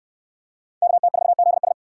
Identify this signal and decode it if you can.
Morse code, heard as a keyed tone, about 65 wpm, 700 Hz. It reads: KT4ZS